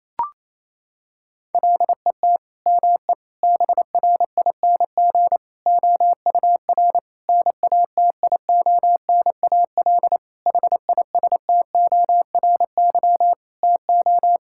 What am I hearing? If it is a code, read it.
Morse code, heard as a keyed tone, about 27 wpm, 690 Hz. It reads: LET ME BRING OUR NATIONAL HISTORY TO
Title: Sentences from Top 1000 Words - Review Part 6 - 28wpm